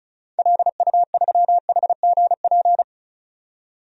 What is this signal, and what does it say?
Morse code, heard as a keyed tone, about 36 wpm, 700 Hz. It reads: LU3HGP
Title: DXCC-100 Call Signs - 6 Characters - 35wpm